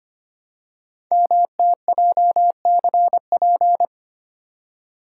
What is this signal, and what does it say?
Morse code, heard as a keyed tone, about 25 wpm, 700 Hz. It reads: MTJCP
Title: Code Groups - 5 Letters - 25wpm